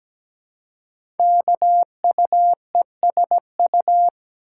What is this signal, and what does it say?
Morse code, heard as a keyed tone, about 17 wpm, 700 Hz. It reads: KUESU
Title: Code Groups - 5 Letter:Numbers - 17wpm